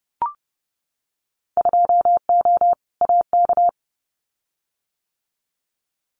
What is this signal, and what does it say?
Morse code, heard as a keyed tone, about 30 wpm, 700 Hz. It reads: 2O AK
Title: Winter Field Day Exchanges Encore 30wpm